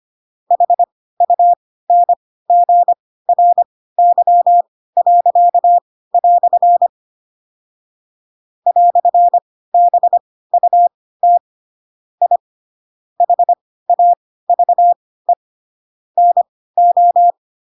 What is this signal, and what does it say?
Morse code, heard as a keyed tone, about 25 wpm, spaced as for 16 wpm, 700 Hz. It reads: HUNGRY." "BUT I HAVE NO
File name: Pinocchio / Pinocchio ch7 10 25 ik1hge